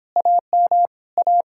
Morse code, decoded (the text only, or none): AM A